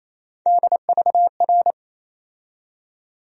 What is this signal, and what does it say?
Morse code, heard as a keyed tone, about 27 wpm, 700 Hz. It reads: DVR
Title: USA Call Sign Postfixes - 3 Letters 28wpm